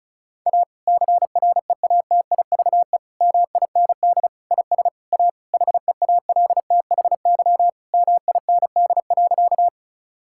Morse code, heard as a keyed tone, about 36 wpm, 690 Hz. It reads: A CREATIVE MIND IS A HEALTHY MIND.
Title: Sentences from Top 1000 Words - Review Part 5 - 35wpm